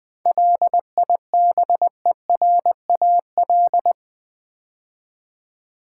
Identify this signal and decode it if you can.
Morse code, heard as a keyed tone, about 20 wpm, 700 Hz. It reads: LIBERAL
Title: 7 Letter Words - First 3 Letters Given - 20wpm